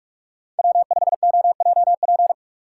Morse code, heard as a keyed tone, about 42 wpm, 700 Hz. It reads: W5OJP
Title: USA Call Signs - 6x Character Spacing - 45wpm